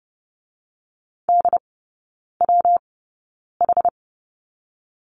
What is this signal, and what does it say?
Morse code, heard as a keyed tone, about 30 wpm, 700 Hz. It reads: D W H
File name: Lesson 19e - Letter w - Three Characters 30wpm